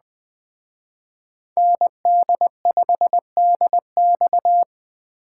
Morse code, heard as a keyed tone, about 22 wpm, 700 Hz. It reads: ND5DX